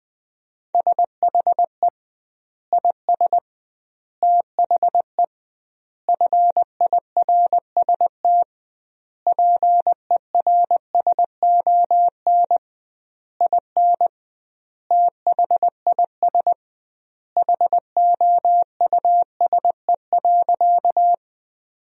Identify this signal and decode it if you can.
Morse code, heard as a keyed tone, about 20 wpm, 700 Hz. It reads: SHE IS THE FIRST PERSON IN THIS HOUSE.
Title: 2x Word-Spacing - Sentences from Top 400 Words - 20wpm